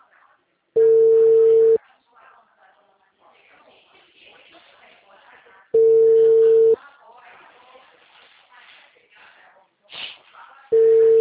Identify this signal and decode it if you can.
Ringback tone (China, 450 Hz), with speech.
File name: out- -2013- - - .60056